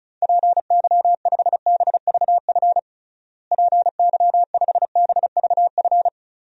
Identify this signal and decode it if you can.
Morse code, heard as a keyed tone, about 34 wpm, 700 Hz. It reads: PY5BVF PY5BVF
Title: DXCC-100 Call Signs - 6 Characters - Repeated 3x - 35wpm